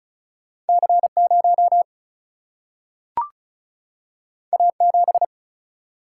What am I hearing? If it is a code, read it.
Morse code, heard as a keyed tone, about 36 wpm, 700 Hz. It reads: C 0 A 7
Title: Single-Letter Single-Number 35wpm